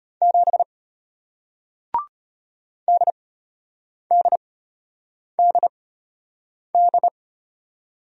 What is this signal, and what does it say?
Morse code, heard as a keyed tone, about 34 wpm, spaced as for 8 wpm, 700 Hz. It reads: 7 DDDD